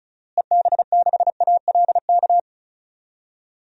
Morse code, heard as a keyed tone, about 36 wpm, 700 Hz. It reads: EB6ALK
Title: DXCC-100 Call Signs - 6 Characters - Repeated 3x - 35wpm